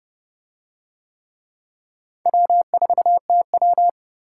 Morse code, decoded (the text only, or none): W4TW